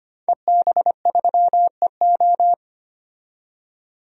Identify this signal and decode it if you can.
Morse code, heard as a keyed tone, about 25 wpm, 700 Hz. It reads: EB3EO
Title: DXCC-100 Call Signs - 5 Characters - 25wpm